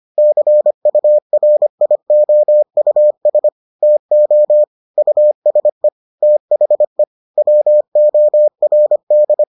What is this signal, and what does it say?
Morse code, heard as a keyed tone, about 25 wpm, 600 Hz. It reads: CURIOUS TO USE THE WORD